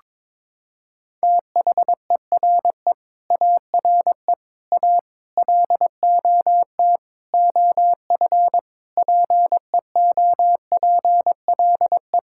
Morse code, heard as a keyed tone, about 22 wpm, 700 Hz. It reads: THERE ARE A LOT OF PEOPLE
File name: practice / 22wpm / Sets of 8 Words - 22wpm